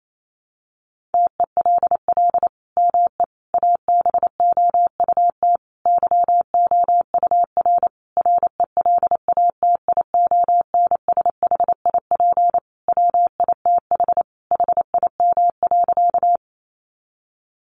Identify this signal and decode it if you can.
Morse code, heard as a keyed tone, about 28 wpm, 700 Hz. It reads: TELL ME ABOUT YOUR RELATIONSHIP WITH HIM.
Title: Sentences from Top 1100 Words 28wpm